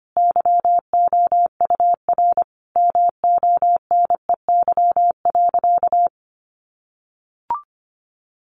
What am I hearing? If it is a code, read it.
Morse code, heard as a keyed tone, about 25 wpm, 700 Hz. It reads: YOUR MONEY.
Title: Sentences from Top 1100 Words 25wpm